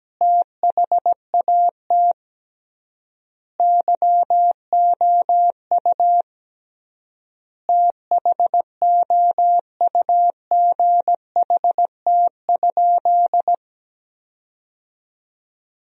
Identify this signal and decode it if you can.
Morse code, heard as a keyed tone, about 17 wpm, 700 Hz. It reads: THAT YOU THOUGHT?
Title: 3x Word-Spacing - Sentences from Top 300 Words - 17wpm